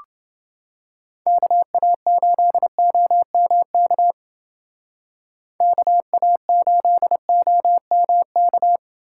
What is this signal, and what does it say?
Morse code, heard as a keyed tone, about 28 wpm, 700 Hz. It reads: KA8OMK KA8OMK